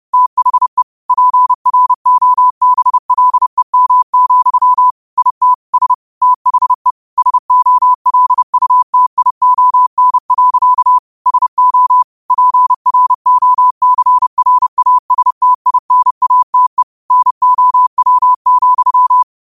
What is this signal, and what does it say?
Morse code, heard as a keyed tone, about 30 wpm, 1000 Hz. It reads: THE PROBLEM, IT S THE SOLUTION. SO PROCRASTINATE NOW,
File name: qotdfiles / 30wpm